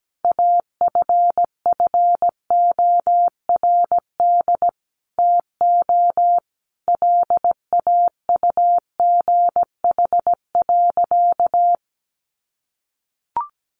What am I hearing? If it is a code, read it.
Morse code, heard as a keyed tone, about 17 wpm, 700 Hz. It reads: AFFORD TO LAUGH.
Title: Sentences from Top 1100 Words 17wpm